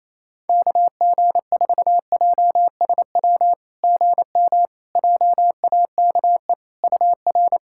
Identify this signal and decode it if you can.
Morse code, heard as a keyed tone, about 28 wpm, 700 Hz. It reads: KG4JSW GM JAKE UR
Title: SOTA QSO Practice 28wpm